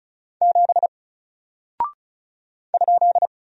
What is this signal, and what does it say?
Morse code, heard as a keyed tone, about 36 wpm, 700 Hz. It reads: 7 ?
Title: Single Character (Letter:Number:Punctuation:Prosign) Alt 35wpm